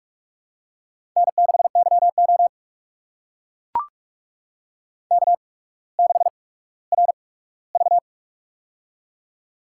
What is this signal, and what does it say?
Morse code, heard as a keyed tone, about 44 wpm, spaced as for 19 wpm, 690 Hz. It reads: N6YO K6RV
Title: USA Call Signs - 4 Characters - 8x Character Spacing - 45wpm